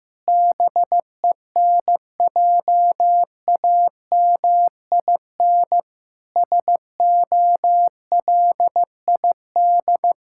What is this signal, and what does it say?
Morse code, heard as a keyed tone, about 15 wpm, 700 Hz. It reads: BENJAMIN SOLID
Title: Practice QSO - Exchange 3 - More Info - 15wpm